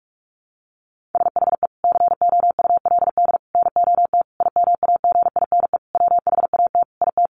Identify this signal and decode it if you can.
Morse code, heard as a keyed tone, about 44 wpm, 700 Hz. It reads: SHE COULD NOT IMAGINE WHAT IT
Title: Sentences from Top 1400 Words 45wpm